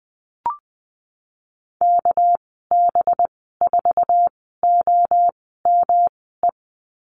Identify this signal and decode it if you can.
Morse code, heard as a keyed tone, about 20 wpm, spaced as for 15 wpm, 700 Hz. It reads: KB4OME
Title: 200 Lesson 32g - Letter k - Call Signs 20wpm